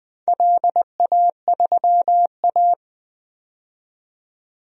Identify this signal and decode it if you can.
Morse code, heard as a keyed tone, about 20 wpm, 700 Hz. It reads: LA3A
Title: DXCC-100 Call Signs - 4 Characters - 20wpm